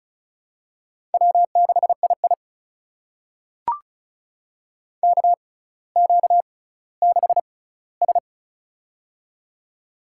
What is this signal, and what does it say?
Morse code, heard as a keyed tone, about 36 wpm, spaced as for 18 wpm, 700 Hz. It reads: W6II KQ6S
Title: USA Call Signs - 4 Characters - 6x Character Spacing - 35wpm